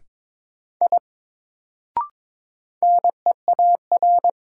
Morse code, heard as a keyed tone, about 22 wpm, 700 Hz. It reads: I NEAR
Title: Sentences from Top 200 Words - New Word Drill - 22wpm